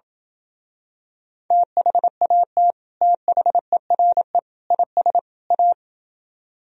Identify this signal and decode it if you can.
Morse code, heard as a keyed tone, about 32 wpm, 700 Hz. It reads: THAT THERE IS A